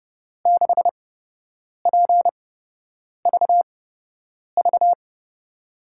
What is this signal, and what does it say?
Morse code, heard as a keyed tone, about 32 wpm, spaced as for 8 wpm, 700 Hz. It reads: 6PVV
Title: USA Call Signs - 6 Characters - 8x Character Spacing - 30wpm